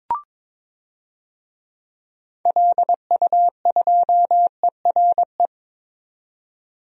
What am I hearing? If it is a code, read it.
Morse code, heard as a keyed tone, about 22 wpm, 700 Hz. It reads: LU2ERE
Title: DXCC-100 Call Signs - 6 Characters - 22wpm